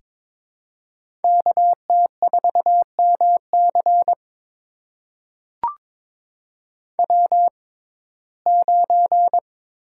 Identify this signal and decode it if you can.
Morse code, heard as a keyed tone, about 22 wpm, spaced as for 16 wpm, 700 Hz. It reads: KT4MC W9
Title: USA Call Signs - 5 Characters - 6x Character Spacing - 22wpm